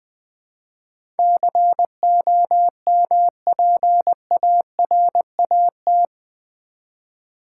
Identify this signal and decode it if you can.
Morse code, heard as a keyed tone, about 20 wpm, 700 Hz. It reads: COMPARAT
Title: Build A Word - 4 Letters At A Time - 11-Letter Words - 20wpm